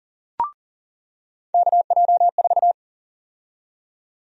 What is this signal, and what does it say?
Morse code, heard as a keyed tone, about 40 wpm, 700 Hz. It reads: KJ4
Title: USA Call Sign Prefixes - 40wpm